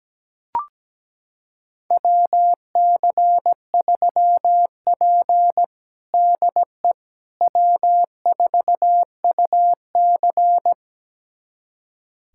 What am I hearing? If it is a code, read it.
Morse code, heard as a keyed tone, about 17 wpm, 700 Hz. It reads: WC3P DE W4UC
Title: USA Callsign DE Callsign - 4 Characters - 17wpm